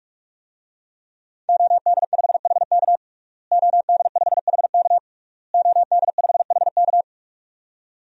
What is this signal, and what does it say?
Morse code, heard as a keyed tone, about 46 wpm, 700 Hz. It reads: OD5HK OD5HK OD5HK
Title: DXCC-100 Call Signs - 5 Characters - Repeated 3x - 45wpm